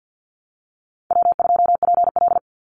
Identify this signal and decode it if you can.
Morse code, heard as a keyed tone, about 49 wpm, 700 Hz. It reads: W2PL
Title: Call Signs Repeated 3x - 50wpm